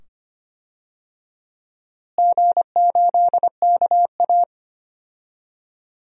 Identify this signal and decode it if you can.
Morse code, heard as a keyed tone, about 25 wpm, 700 Hz. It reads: G8KA
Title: DXCC-100 Call Signs - 4 Characters - Repeated 2x - 25wpm